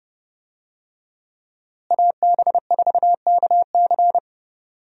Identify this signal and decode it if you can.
Morse code, heard as a keyed tone, about 30 wpm, 700 Hz. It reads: AB4KC
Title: USA Call Signs - 5 Characters - Repeated 2x - 30wpm